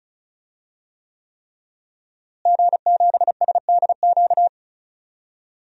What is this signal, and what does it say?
Morse code, heard as a keyed tone, about 34 wpm, 700 Hz. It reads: G7SDQ
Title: DXCC-100 Call Signs - 5 Characters - Repeated 3x - 35wpm